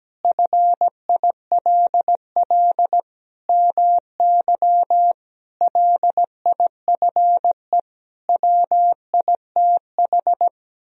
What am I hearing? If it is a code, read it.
Morse code, heard as a keyed tone, about 17 wpm, 700 Hz. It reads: FILL MY LIFE WITH
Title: Sentences from Top 1000 Words - Review Part 3 - 17wpm